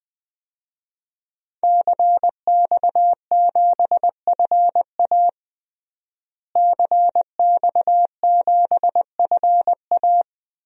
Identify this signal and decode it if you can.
Morse code, heard as a keyed tone, about 21 wpm, 700 Hz. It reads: CX7FA CX7FA
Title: DXCC-100 Call Signs - 5 Characters - Repeated 3x - 20wpm